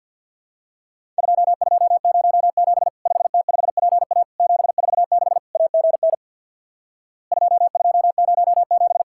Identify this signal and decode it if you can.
Morse code, heard as a keyed tone, about 50 wpm, 700 Hz. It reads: 2207 5T5PA 746 AGN 2207
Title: Worked All Europe DX Contest - QTC Exchange with Repeats - 50wpm